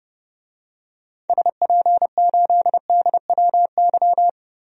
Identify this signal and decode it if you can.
Morse code, heard as a keyed tone, about 32 wpm, 700 Hz. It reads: SP8DWY
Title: DXCC-100 Call Signs - 6 Characters - 30wpm